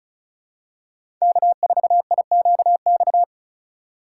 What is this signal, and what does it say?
Morse code, heard as a keyed tone, about 34 wpm, 700 Hz. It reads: K4IQX
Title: USA Call Signs - 5 Characters - 2x Character Spacing - 35wpm